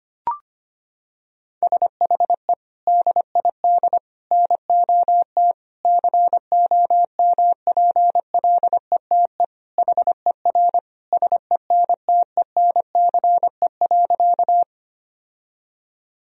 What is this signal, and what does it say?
Morse code, heard as a keyed tone, about 25 wpm, 700 Hz. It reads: SHE DID NOT COMPLETE HER SENTENCE.
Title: Sentences from Top 1000 Words - Review Part 2 - 25wpm